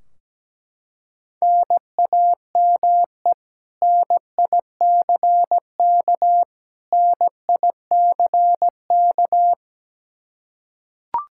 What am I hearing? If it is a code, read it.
Morse code, heard as a keyed tone, about 17 wpm, 700 Hz. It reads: NAME NICK NICK
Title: Practice QSO - Exchange 1 - Beginning - 17wpm